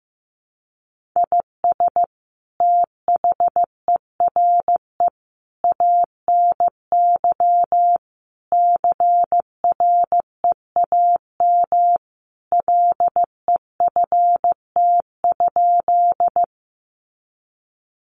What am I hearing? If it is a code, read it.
Morse code, heard as a keyed tone, about 15 wpm, 700 Hz. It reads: IS THERE ANY CREAM LEFT?
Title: Sentences from Top 1500 Words 15wpm